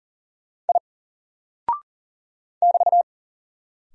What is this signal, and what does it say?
Morse code, heard as a keyed tone, about 39 wpm, 700 Hz. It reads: I <BT>
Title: Single Character (Letter:Number:Punctuation:Prosign) 40wpm